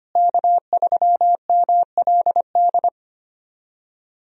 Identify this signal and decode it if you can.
Morse code, heard as a keyed tone, about 25 wpm, 700 Hz. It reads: K3MLD
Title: USA Call Signs - 5 Characters - Repeated 2x - 25wpm